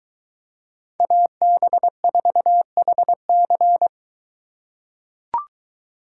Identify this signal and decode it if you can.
Morse code, heard as a keyed tone, about 22 wpm, 700 Hz. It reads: AB4HC